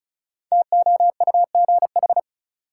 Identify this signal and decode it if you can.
Morse code, heard as a keyed tone, about 34 wpm, 700 Hz. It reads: TOUGH
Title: Sentences from Top 1200 Words - New Word Drill - 35wpm